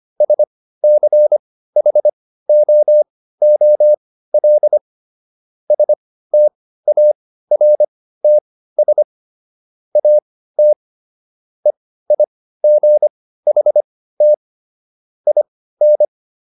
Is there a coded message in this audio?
Yes: Morse code from 0.2 to 16.1 s, heard as a keyed tone, about 25 wpm, spaced as for 15 wpm, 600 Hz. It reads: SCHOOL STARTS AT EIGHT IN